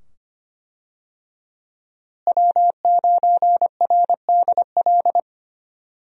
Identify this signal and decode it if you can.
Morse code, heard as a keyed tone, about 25 wpm, 700 Hz. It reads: W9RDL
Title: USA Call Signs - 5 Characters - Repeated 2x - 25wpm